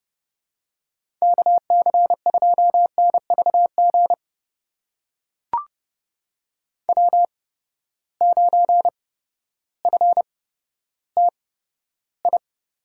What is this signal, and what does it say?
Morse code, heard as a keyed tone, about 30 wpm, spaced as for 14 wpm, 700 Hz. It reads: KC2NVG W9FTI
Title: USA Call Signs - 8x Character Spacing - 30wpm